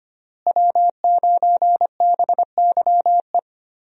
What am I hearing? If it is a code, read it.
Morse code, heard as a keyed tone, about 25 wpm, 700 Hz. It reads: W9BYE
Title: USA Call Signs - 5 Characters - Repeated 3x - 25wpm